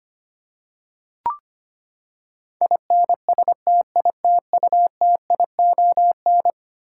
Morse code, heard as a keyed tone, about 25 wpm, 700 Hz. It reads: INSTITUTION
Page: morse-code-ninja-practice-25wpm